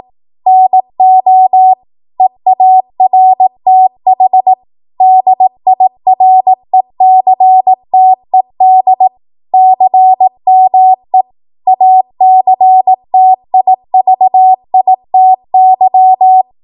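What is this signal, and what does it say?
Morse code, heard as a keyed tone, about 18 wpm, 750 Hz. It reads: NO EARTH DIRECTED CME ACTIVITY
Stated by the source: 18 WPM Morse code practice from a national amateur radio association's code archive.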